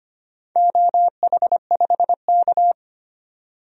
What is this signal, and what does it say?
Morse code, heard as a keyed tone, about 25 wpm, 700 Hz. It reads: OH5K
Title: DXCC-100 Call Signs - 4 Characters - 25wpm